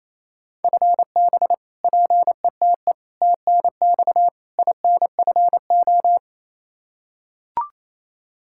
Morse code, heard as a keyed tone, about 28 wpm, 700 Hz. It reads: FB PETE TNX INFO
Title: Practice QSO - Exchange 3 - More Info - 28wpm